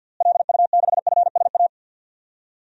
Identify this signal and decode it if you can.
Morse code, heard as a keyed tone, about 47 wpm, 700 Hz. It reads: LU6FSA
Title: DXCC-100 Call Signs - 6 Characters - Repeated 3x - 50wpm